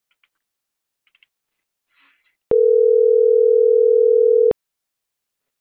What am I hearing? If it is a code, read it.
Ringback tone (North America), 440 + 480 Hz.
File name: CENTRAL MAERSK / out martin.blas